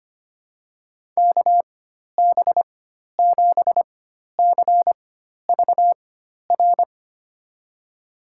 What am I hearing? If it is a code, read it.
Morse code, heard as a keyed tone, about 25 wpm, spaced as for 12 wpm, 700 Hz. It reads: KB7CVR
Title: USA Call Signs - 6 Characters - 4x Character Spacing - 25wpm